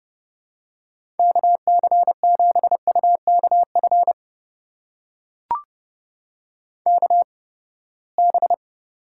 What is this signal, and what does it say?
Morse code, heard as a keyed tone, about 31 wpm, spaced as for 21 wpm, 700 Hz. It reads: KC7UKF KB
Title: USA Call Signs - 6 Characters - 8x Character Spacing - 30wpm